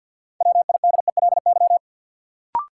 Morse code, heard as a keyed tone, about 53 wpm, 700 Hz. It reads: WIDELY
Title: Sentences from Top 1200 Words - New Word Drill Alt - 50wpm